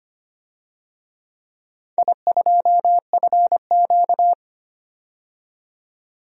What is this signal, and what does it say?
Morse code, heard as a keyed tone, about 25 wpm, 700 Hz. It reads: I2FQ
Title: DXCC-100 Call Signs - 4 Characters - 25wpm